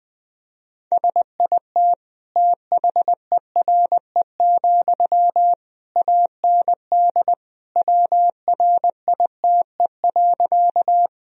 Morse code, heard as a keyed tone, about 20 wpm, 700 Hz. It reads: SIT THERE, AND WRITE.
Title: Sentences from Top 1000 Words - Review Part 1 - 20wpm